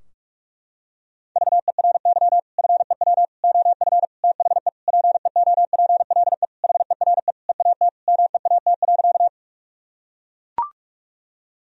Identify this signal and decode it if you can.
Morse code, heard as a keyed tone, about 41 wpm, 700 Hz. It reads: VERY FEW OF THE PEOPLE HERE EAT MEAT.
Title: Sentences from Top 1000 Words - Review Part 3 - 45wpm